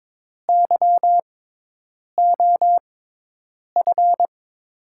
Morse code, heard as a keyed tone, about 22 wpm, spaced as for 8 wpm, 700 Hz. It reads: YOF
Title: USA Call Signs - 6 Characters - 6x Character Spacing - 22wpm